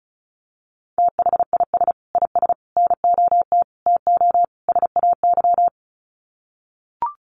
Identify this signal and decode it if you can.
Morse code, heard as a keyed tone, about 34 wpm, 700 Hz. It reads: THIS IS NOT TO SAY
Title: Sets of 5 Words Encore 35wpm